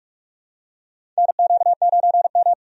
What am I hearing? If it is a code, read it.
Morse code, heard as a keyed tone, about 46 wpm, 700 Hz. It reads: NQ9M
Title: USA Call Signs - 4 Characters - 45wpm